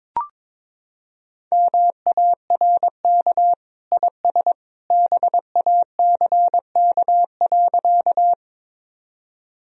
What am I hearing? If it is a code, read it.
Morse code, heard as a keyed tone, about 22 wpm, 700 Hz. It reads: MARK IS BACK.
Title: Sentences from Top 1000 Words - Review Part 1 - 22wpm